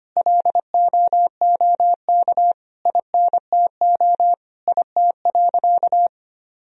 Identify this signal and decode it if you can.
Morse code, heard as a keyed tone, about 25 wpm, 700 Hz. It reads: LOOK INTO IT.
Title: Sentences from Top 1000 Words - Review Part 1 - 25wpm